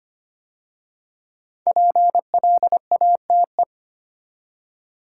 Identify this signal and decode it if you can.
Morse code, heard as a keyed tone, about 25 wpm, 700 Hz. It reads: PLATE
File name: 5 Letter Words - First 3 Letters Given - 25wpm